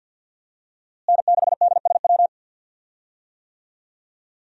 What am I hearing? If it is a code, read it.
Morse code, heard as a keyed tone, about 52 wpm, 690 Hz. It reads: N6DSW
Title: USA Call Signs - 5 Characters - Repeated 3x - 50wpm